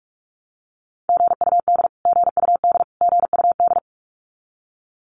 Morse code, heard as a keyed tone, about 44 wpm, 700 Hz. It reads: GUD GUD GUD
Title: Abbreviations Repeated 3x - 45wpm